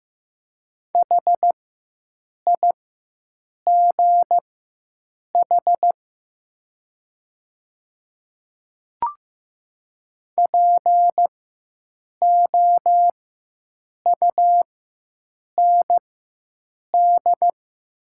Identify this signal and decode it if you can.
Morse code, heard as a keyed tone, about 15 wpm, spaced as for 7 wpm, 690 Hz. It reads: HIGH POUND
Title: 4x Character Spacing - Top 500 Words - 15wpm